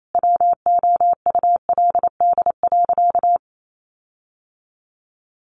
Morse code, heard as a keyed tone, about 28 wpm, 700 Hz. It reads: WOULD.
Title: Sentences from Top 1500 Words 28wpm